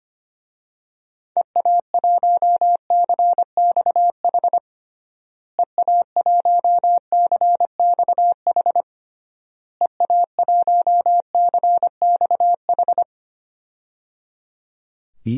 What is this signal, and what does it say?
Morse code, heard as a keyed tone, about 25 wpm, 700 Hz. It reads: EA1CXH EA1CXH EA1CXH
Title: DXCC-100 Call Signs - 6 Characters - Repeated 3x - 25wpm